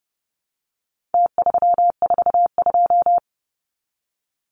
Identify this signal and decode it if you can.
Morse code, heard as a keyed tone, about 30 wpm, 700 Hz. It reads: T342
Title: Numbers - Contest Style - 4-Digit 30wpm